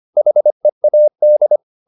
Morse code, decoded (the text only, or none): HEAD